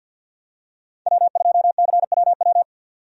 Morse code, heard as a keyed tone, about 50 wpm, 700 Hz. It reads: W2CWW
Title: USA Call Signs - 5 Characters - 50wpm